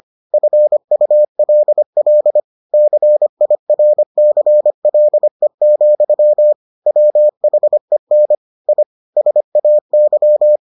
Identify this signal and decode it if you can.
Morse code, heard as a keyed tone, about 25 wpm, 600 Hz. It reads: FULL CIRCLE, WHEN I SAY